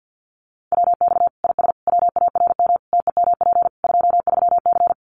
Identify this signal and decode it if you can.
Morse code, heard as a keyed tone, about 50 wpm, 700 Hz. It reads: WX IS WARM TEMP 23C
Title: Common QSO Phrases 50wpm